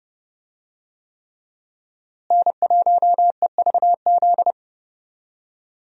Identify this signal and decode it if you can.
Morse code, heard as a keyed tone, about 31 wpm, 700 Hz. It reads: N1EVZ
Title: USA Call Signs - 5 Characters - Repeated 3x - 30wpm